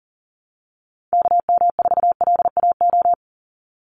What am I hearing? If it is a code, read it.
Morse code, heard as a keyed tone, about 40 wpm, 700 Hz. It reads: KM4LAO
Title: Call Signs Repeated 3x - 40wpm